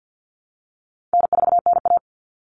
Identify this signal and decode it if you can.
Morse code, heard as a keyed tone, about 50 wpm, 700 Hz. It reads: N4NA
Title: All Cut Numbers - 4-Digit 50wpm